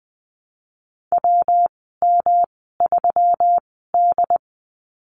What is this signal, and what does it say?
Morse code, heard as a keyed tone, about 20 wpm, spaced as for 14 wpm, 700 Hz. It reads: WM3D
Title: 122 Lesson 20g - Number 3 - Call Signs 20wpm